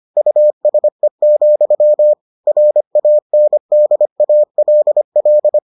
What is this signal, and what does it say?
Morse code, heard as a keyed tone, about 25 wpm, 600 Hz. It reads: USE, RANDALL